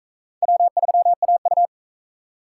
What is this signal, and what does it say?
Morse code, heard as a keyed tone, about 40 wpm, 700 Hz. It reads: W3AU